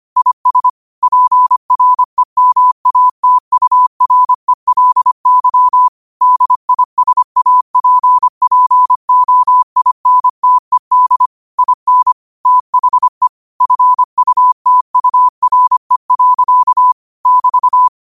Morse code, heard as a keyed tone, about 25 wpm, 1000 Hz. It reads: IS PREMATURELY DISAPPOINTED IN THE FUTURE. <BT>
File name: qotdfiles / 25wpm